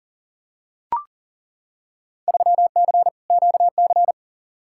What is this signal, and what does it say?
Morse code, heard as a keyed tone, about 40 wpm, 700 Hz. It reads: 3C QC
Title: Field Day Exchanges Encore - 40wpm